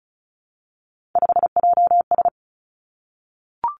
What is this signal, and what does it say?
Morse code, heard as a keyed tone, about 36 wpm, 700 Hz. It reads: 5JS
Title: Three Challenging Letter-Number Combinations 35wpm